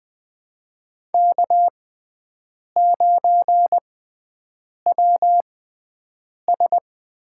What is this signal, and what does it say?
Morse code, heard as a keyed tone, about 20 wpm, spaced as for 7 wpm, 700 Hz. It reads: K9WS
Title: USA Call Signs - 4 Characters - 6x Character Spacing - 20wpm